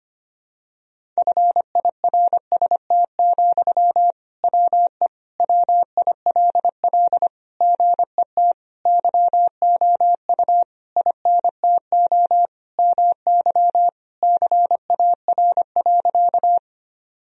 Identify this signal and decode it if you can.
Morse code, heard as a keyed tone, about 25 wpm, 700 Hz. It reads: FIRST, WE WILL GET YOU INTO MY CAR.
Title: Sentences from Top 1000 Words - Review Part 1 - 25wpm